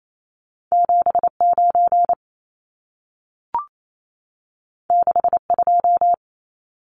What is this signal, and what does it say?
Morse code, heard as a keyed tone, about 28 wpm, 700 Hz. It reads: 79 62
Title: Numbers - 2-Digit 28wpm